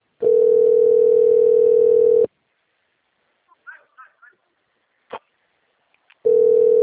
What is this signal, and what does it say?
Ringback tone (North America), 440 + 480 Hz, with speech.
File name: rg-1600-unknown- - - .11007